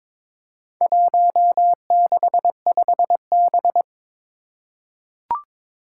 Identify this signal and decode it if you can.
Morse code, heard as a keyed tone, about 22 wpm, 700 Hz. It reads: 165B
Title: Four Challenging Letter-Number Combinations 22wpm